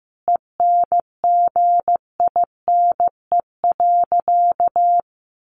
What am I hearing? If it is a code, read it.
Morse code, heard as a keyed tone, about 15 wpm, 700 Hz. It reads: ENGINE.
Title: Sentences from Top 1500 Words 15wpm